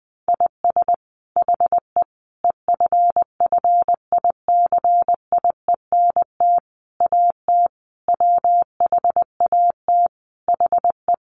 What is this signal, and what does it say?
Morse code, heard as a keyed tone, about 20 wpm, 700 Hz. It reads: IS HE EFFICIENT AT WHAT HE
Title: Sentences from Top 1300 Words 20wpm